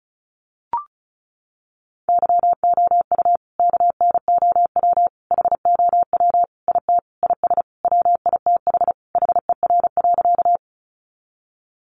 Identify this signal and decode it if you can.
Morse code, heard as a keyed tone, about 34 wpm, 700 Hz. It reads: YOU KNOW HOW IT IS WITH HER.
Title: Sentences from Top 100 Words 35wpm